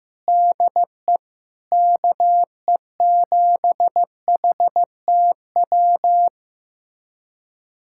Morse code, heard as a keyed tone, about 15 wpm, 700 Hz. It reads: DE KE7HTW
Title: USA Callsign DE Callsign 15wpm